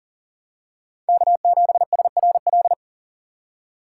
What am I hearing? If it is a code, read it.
Morse code, heard as a keyed tone, about 40 wpm, 700 Hz. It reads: K7SRL
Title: USA Call Signs - 5 Characters - 2x Character Spacing - 40wpm